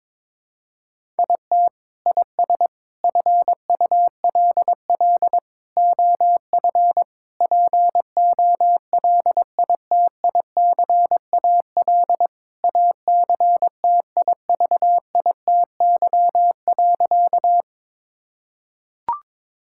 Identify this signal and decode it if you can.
Morse code, heard as a keyed tone, about 22 wpm, 700 Hz. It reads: IT IS FULL OF POLITICAL ACTIVITY.
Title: Sentences from Top 1000 Words - Review Part 5 - 22wpm